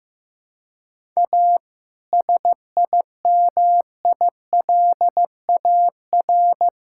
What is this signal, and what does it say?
Morse code, heard as a keyed tone, about 15 wpm, 700 Hz. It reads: A SIMILAR
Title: Sentences from Top 1000 Words - Review Part 5 - 15wpm